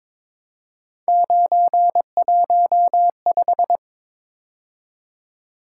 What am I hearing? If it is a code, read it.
Morse code, heard as a keyed tone, about 22 wpm, 700 Hz. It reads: 915
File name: practice / 22wpm / Three Letter-Number Combinations - 22wpm